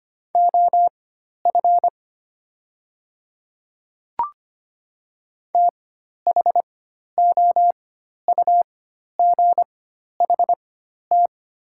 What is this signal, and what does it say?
Morse code, heard as a keyed tone, about 25 wpm, spaced as for 12 wpm, 700 Hz. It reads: OF THOUGHT
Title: 4x Character Spacing - Top 300 Words - 25wpm